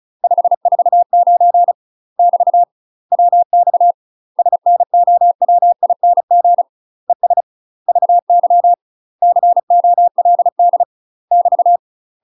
Morse code, heard as a keyed tone, about 36 wpm, 700 Hz. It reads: 549 <BT> WX SNOWING ES VY COLD <BT>
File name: UK MASTER QSOs TRAINING / 35WPM / UK MASTER QSOs 0031